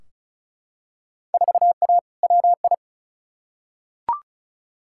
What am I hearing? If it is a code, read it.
Morse code, heard as a keyed tone, about 36 wpm, 700 Hz. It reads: 4A WI
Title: Field Day Exchanges - 35wpm